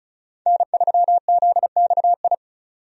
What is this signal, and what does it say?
Morse code, heard as a keyed tone, about 34 wpm, 700 Hz. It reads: N3ZXI
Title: USA Call Signs - 5 Characters - 35wpm